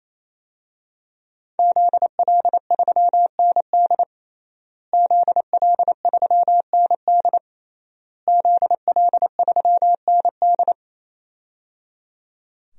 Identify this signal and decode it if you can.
Morse code, heard as a keyed tone, about 28 wpm, 700 Hz. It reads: ZL3ND ZL3ND ZL3ND
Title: DXCC-100 Call Signs - 5 Characters - Repeated 3x - 28wpm